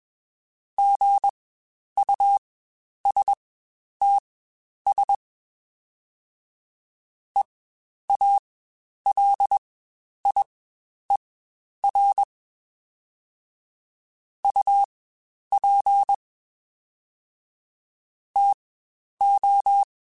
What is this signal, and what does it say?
Morse code, heard as a keyed tone, about 21 wpm, spaced as for 10 wpm, 800 Hz. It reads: GUSTS EALIER UP TO
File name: Spaced Characters / Spaced Characters-mp3s / Weather5 5 sp @20